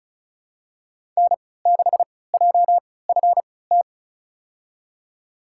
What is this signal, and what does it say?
Morse code, heard as a keyed tone, about 34 wpm, spaced as for 20 wpm, 700 Hz. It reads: N6JFT
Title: USA Call Signs - 5 Characters - 3x Character Spacing - 35wpm